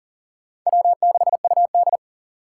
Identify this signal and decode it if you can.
Morse code, heard as a keyed tone, about 37 wpm, 700 Hz. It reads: W6UD
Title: USA Call Signs - 6x Character Spacing - 40wpm